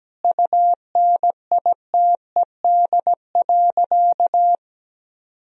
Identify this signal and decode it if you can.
Morse code, heard as a keyed tone, about 17 wpm, 700 Hz. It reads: UNITED.
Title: Sentences from Top 1000 Words - Review Part 4 - 17wpm